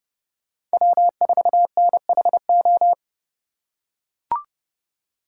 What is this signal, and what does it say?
Morse code, heard as a keyed tone, about 30 wpm, 700 Hz. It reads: W4NHO
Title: USA Call Signs - 5 Characters - 3x Character Spacing - 30wpm